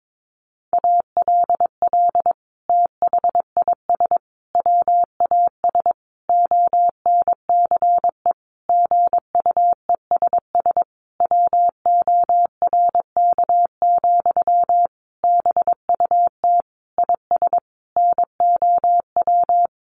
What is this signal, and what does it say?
Morse code, heard as a keyed tone, about 22 wpm, 700 Hz. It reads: ALL THIS WAS ONCE GUESS WORK, BUT IS NOW
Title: Sentences from Top 1500 Words 22wpm